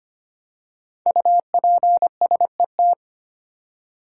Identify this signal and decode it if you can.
Morse code, heard as a keyed tone, about 25 wpm, 700 Hz. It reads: UPSET
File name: 5 Letter Words - First 3 Letters Given - 25wpm